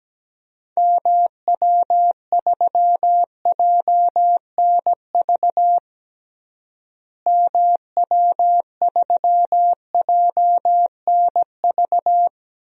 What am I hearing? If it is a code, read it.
Morse code, heard as a keyed tone, about 17 wpm, 700 Hz. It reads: MW3JNV MW3JNV
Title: DXCC-100 Call Signs - 6 Characters - Repeated 3x - 17wpm